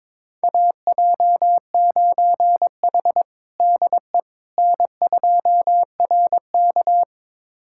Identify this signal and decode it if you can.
Morse code, heard as a keyed tone, about 22 wpm, 700 Hz. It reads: AJ9H DE N2RK
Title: USA Callsign DE Callsign - 4 Characters - 22wpm